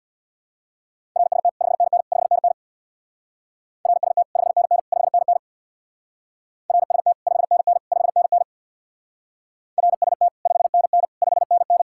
Morse code, heard as a keyed tone, about 58 wpm, 700 Hz. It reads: RST 5NN 5NN RST 5NN 5NN RST 5NN 5NN RST 5NN 5NN
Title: Speed-Racing Common QSO Phrases - 50wpm